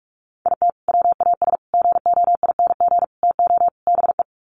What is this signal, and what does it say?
Morse code, heard as a keyed tone, about 44 wpm, 700 Hz. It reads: IT WAS GOING TO BE
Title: Sets of 5 Words 45wpm